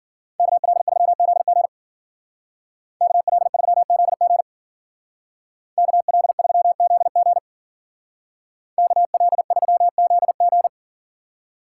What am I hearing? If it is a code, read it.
Morse code, heard as a keyed tone, about 53 wpm, 700 Hz. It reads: KL3ZG KL3ZG KL3ZG KL3ZG